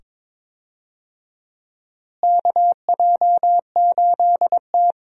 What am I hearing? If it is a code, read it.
Morse code, heard as a keyed tone, about 22 wpm, 700 Hz. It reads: KJ8T
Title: USA Call Signs - 4 Characters - Repeated 2x - 22wpm